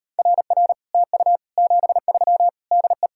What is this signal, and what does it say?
Morse code, heard as a keyed tone, about 38 wpm, 700 Hz. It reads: RR TU 73 DE